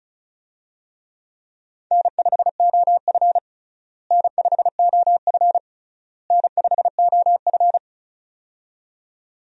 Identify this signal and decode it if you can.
Morse code, heard as a keyed tone, about 34 wpm, 700 Hz. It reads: N5OF N5OF N5OF
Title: USA Call Signs - 4 Characters - Repeated 3x - 35wpm